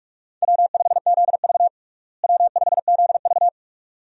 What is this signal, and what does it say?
Morse code, heard as a keyed tone, about 46 wpm, 700 Hz. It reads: W5ZV W5ZV
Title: USA Call Signs - 4 Characters - Repeated 2x - 45wpm